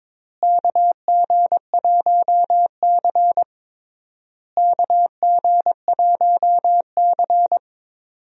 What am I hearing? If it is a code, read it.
Morse code, heard as a keyed tone, about 22 wpm, 700 Hz. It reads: KG1C KG1C
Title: USA Call Signs - 4 Characters - Repeated 3x - 22wpm